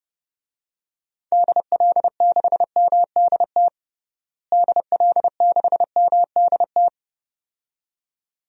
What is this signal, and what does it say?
Morse code, heard as a keyed tone, about 30 wpm, 700 Hz. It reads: DL6MDT DL6MDT
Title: DXCC-100 Call Signs - 6 Characters - Repeated 2x - 30wpm